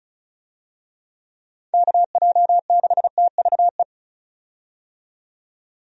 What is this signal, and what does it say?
Morse code, heard as a keyed tone, about 34 wpm, 700 Hz. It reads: KJ6TVE
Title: USA Call Signs - 6 Characters - 35wpm